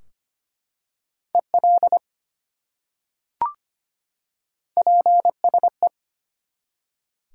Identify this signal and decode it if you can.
Morse code, heard as a keyed tone, about 25 wpm, 700 Hz. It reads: EL PSE
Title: Top 180 QSO Elements 25wpm